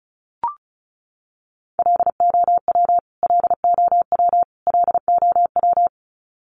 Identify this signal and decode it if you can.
Morse code, heard as a keyed tone, about 34 wpm, 700 Hz. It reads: LOW LOW LOW
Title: Top 300 Words Repeated 35wpm